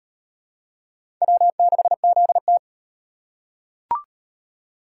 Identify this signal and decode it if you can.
Morse code, heard as a keyed tone, about 38 wpm, 700 Hz. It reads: W6ZT